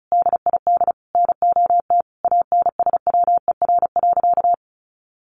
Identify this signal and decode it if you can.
Morse code, heard as a keyed tone, about 34 wpm, 700 Hz. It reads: DID NOT ANSWER.
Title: Sentences from Top 600 Words 35wpm